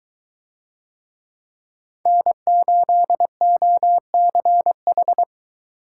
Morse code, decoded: N8OCH

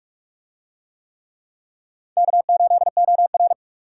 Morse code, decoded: K8OR

